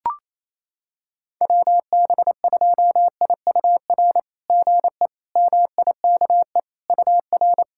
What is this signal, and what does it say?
Morse code, heard as a keyed tone, about 28 wpm, 700 Hz. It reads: WB2IUR GE MIKE UR
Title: SOTA QSO Practice 28wpm